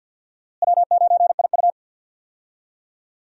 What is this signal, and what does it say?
Morse code, heard as a keyed tone, about 52 wpm, 690 Hz. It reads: W9IU
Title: USA Call Signs - 4 Characters - Repeated 2x - 50wpm